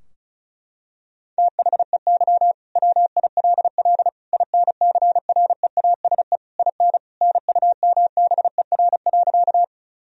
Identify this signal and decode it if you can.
Morse code, heard as a keyed tone, about 36 wpm, 700 Hz. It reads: THEY WILL INCREASE IN NUMBER.
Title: Sentences from Top 1000 Words - Review Part 4 - 35wpm